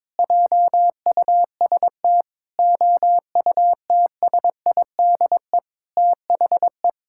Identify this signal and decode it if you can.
Morse code, heard as a keyed tone, about 22 wpm, 700 Hz. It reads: JUST OUTSIDE THE
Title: Sentences from Top 1000 Words - Review Part 6 - 22wpm